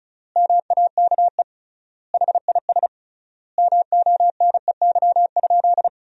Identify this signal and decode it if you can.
Morse code, heard as a keyed tone, about 34 wpm, 700 Hz. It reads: MAKE HIS MONEY?
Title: 3x Word-Spacing - Sentences from Top 400 Words - 35wpm